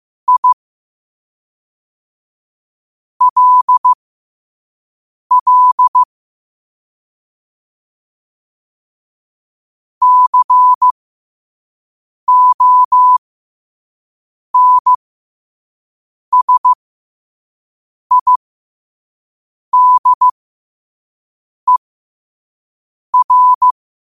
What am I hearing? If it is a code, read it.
Morse code, heard as a keyed tone, about 15 wpm, spaced as for 5 wpm, 1000 Hz. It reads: I LL CONSIDER
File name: qotdfiles / 5wpm